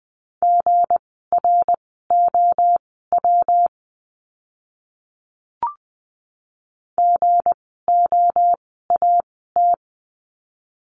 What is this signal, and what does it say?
Morse code, heard as a keyed tone, about 19 wpm, spaced as for 14 wpm, 700 Hz. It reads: GROW GOAT
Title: 160 Lesson 26f - Letter g - Words 20wpm